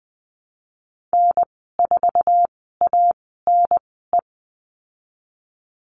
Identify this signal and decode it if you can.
Morse code, heard as a keyed tone, about 20 wpm, spaced as for 14 wpm, 700 Hz. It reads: N4ANE
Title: 069 Lesson 12g - Letter d - Call Signs 20wpm